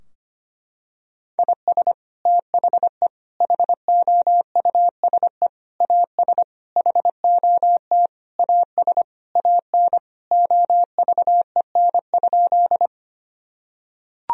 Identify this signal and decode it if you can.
Morse code, heard as a keyed tone, about 25 wpm, 700 Hz. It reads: IS THE HOUSE AS HOT AS AN OVEN?
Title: Sentences from Top 1000 Words - Review Part 4 - 25wpm